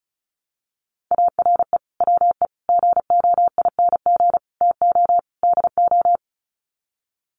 Morse code, heard as a keyed tone, about 34 wpm, 700 Hz. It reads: ARE WE GOING TO DO
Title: Sets of 5 Words 35wpm